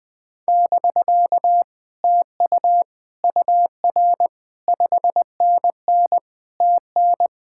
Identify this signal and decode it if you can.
Morse code, heard as a keyed tone, about 20 wpm, 700 Hz. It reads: <BK> TU UR 5NN TN
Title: Speed-Racing Common QSO Phrases - 17wpm